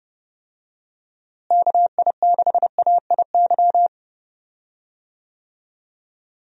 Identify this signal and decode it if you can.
Morse code, heard as a keyed tone, about 31 wpm, 700 Hz. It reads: KI6AIY
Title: USA Call Signs - 6 Characters - Repeated 3x - 30wpm